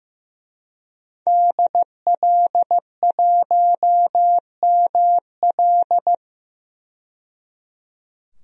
Morse code, heard as a keyed tone, about 15 wpm, 700 Hz. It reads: DL1ML
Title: DXCC-100 Call Signs - 5 Characters - 15wpm